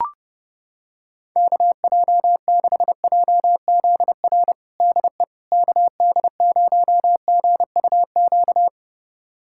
Morse code, heard as a keyed tone, about 30 wpm, 700 Hz. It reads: KJ6JZR DE KD0GUQ
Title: USA Callsign DE Callsign 30wpm